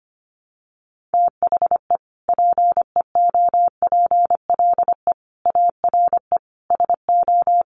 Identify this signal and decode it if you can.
Morse code, heard as a keyed tone, about 25 wpm, 700 Hz. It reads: THE PEOPLE ARE SO
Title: Sentences from Top 1000 Words 25wpm